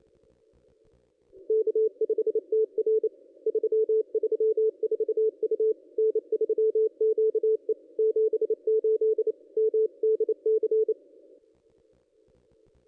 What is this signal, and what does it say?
Morse code, heard as a keyed tone, about 29 wpm, 440 Hz. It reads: K5TR 334 U N3QE 78 MDC
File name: contests / ss-cw-k5tr-op-n5rz -1400